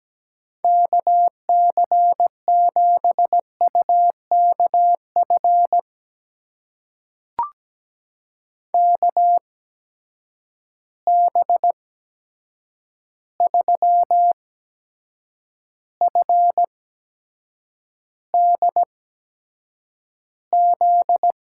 Morse code, heard as a keyed tone, about 17 wpm, spaced as for 7 wpm, 700 Hz. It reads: KC7UKF KB3FDZ
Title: USA Call Signs - 6 Characters - 8x Character Spacing - 17wpm